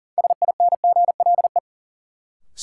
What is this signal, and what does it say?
Morse code, heard as a keyed tone, about 40 wpm, 700 Hz. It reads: SINGLE
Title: Sentences from Top 700 Words - New Word Drill - 40wpm